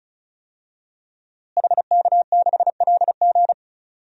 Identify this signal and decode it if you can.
Morse code, heard as a keyed tone, about 36 wpm, 700 Hz. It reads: HK6LG
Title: DXCC-100 Call Signs - 5 Characters - Repeated 3x - 35wpm